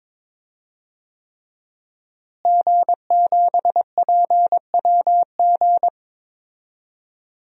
Morse code, heard as a keyed tone, about 22 wpm, 700 Hz. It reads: G7PWG
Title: DXCC-100 Call Signs - 5 Characters - Repeated 3x - 22wpm